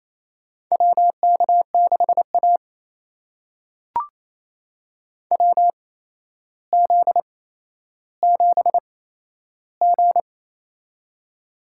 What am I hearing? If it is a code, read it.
Morse code, heard as a keyed tone, about 28 wpm, spaced as for 12 wpm, 700 Hz. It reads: WK6A WZ7G
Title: USA Call Signs - 4 Characters - 8x Character Spacing - 28wpm